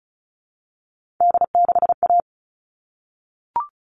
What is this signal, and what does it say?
Morse code, heard as a keyed tone, about 34 wpm, 700 Hz. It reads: D6A
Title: All Cut Numbers - 3-Digit 35wpm